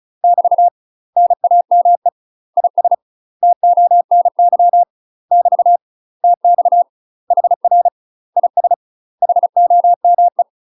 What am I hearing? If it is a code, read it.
Morse code, heard as a keyed tone, about 36 wpm, 700 Hz. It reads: <BT> NAME IS TONY <BT> TX HR IS HOME
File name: UK MASTER QSOs TRAINING / 35WPM / UK MASTER QSOs 0029